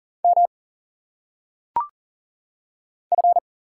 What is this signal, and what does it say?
Morse code, heard as a keyed tone, about 40 wpm, 700 Hz. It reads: M F
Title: Single Letters - Phonetic Pronunciation 40wpm